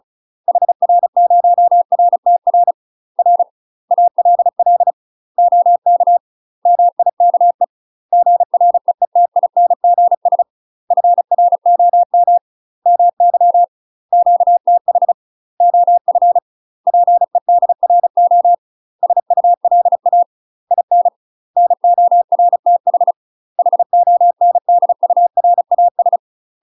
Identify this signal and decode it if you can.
Morse code, heard as a keyed tone, about 34 wpm, 700 Hz. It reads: HR0RTR R ALL OK MIKE GREETINGS FROM MY QTH OF PEDRO SULA IN NORTH HONDURAS
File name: UK MASTER QSOs TRAINING / 35WPM / UK MASTER QSOs 0029